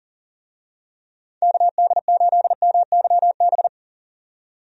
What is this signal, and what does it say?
Morse code, heard as a keyed tone, about 38 wpm, 700 Hz. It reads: KD8MYB
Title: USA Call Signs - 6 Characters - 40wpm